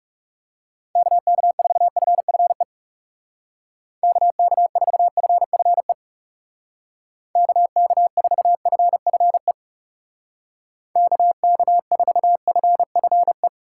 Morse code, heard as a keyed tone, about 39 wpm, 700 Hz. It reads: KK4FFE KK4FFE KK4FFE KK4FFE